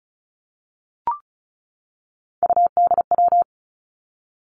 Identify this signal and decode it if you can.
Morse code, heard as a keyed tone, about 36 wpm, 700 Hz. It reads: UDW
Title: Three Challenging Letter-Number Combinations 35wpm